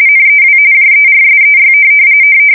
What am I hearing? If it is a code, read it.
RTTY, about 45.45 baud. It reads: NEW JERSEY
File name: fsk170 / USS-New-Jersey